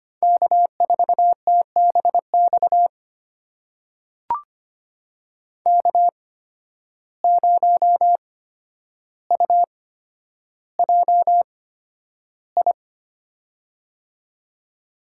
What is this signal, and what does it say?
Morse code, heard as a keyed tone, about 25 wpm, spaced as for 11 wpm, 700 Hz. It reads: K4TBX K0UJI
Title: USA Call Signs - 5 Characters - 8x Character Spacing - 25wpm